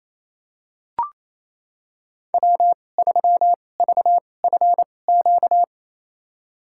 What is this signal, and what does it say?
Morse code, heard as a keyed tone, about 28 wpm, spaced as for 20 wpm, 700 Hz. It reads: W3VFQ
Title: USA Call Signs - 5 Characters - 2x Character Spacing - 28wpm